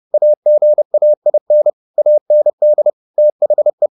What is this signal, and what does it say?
Morse code, heard as a keyed tone, about 30 wpm, 600 Hz. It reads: AGAIN AND THE
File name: the old man and the sea / the old man and the sea 1 30wpm